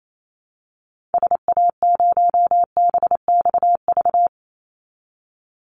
Morse code, heard as a keyed tone, about 28 wpm, 700 Hz. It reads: SA0BXV
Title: Call Signs Repeated 3x - 28wpm